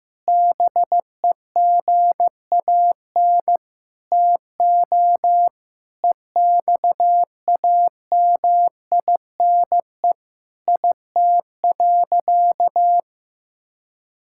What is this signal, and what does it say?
Morse code, heard as a keyed tone, about 16 wpm, 700 Hz. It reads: BEGAN TO EXAMINE IT.
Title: Sentences from Top 1600 Words 15wpm